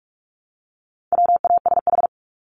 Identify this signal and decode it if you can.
Morse code, heard as a keyed tone, about 44 wpm, 700 Hz. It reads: WASH
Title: Top 1751-2000 Words 45wpm